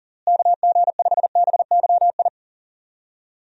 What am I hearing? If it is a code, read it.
Morse code, heard as a keyed tone, about 40 wpm, 700 Hz. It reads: KG5BYI
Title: USA Call Signs - 6 Characters - Repeated 3x - 40wpm